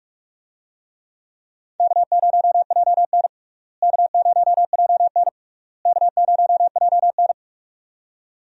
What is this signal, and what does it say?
Morse code, heard as a keyed tone, about 44 wpm, 700 Hz. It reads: K0JN K0JN K0JN
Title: USA Call Signs - 4 Characters - Repeated 3x - 45wpm